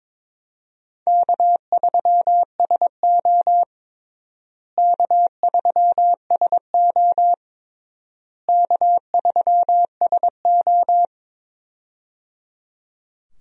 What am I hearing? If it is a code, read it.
Morse code, heard as a keyed tone, about 22 wpm, 700 Hz. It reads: K3SO K3SO K3SO
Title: USA Call Signs - 4 Characters - Repeated 3x - 22wpm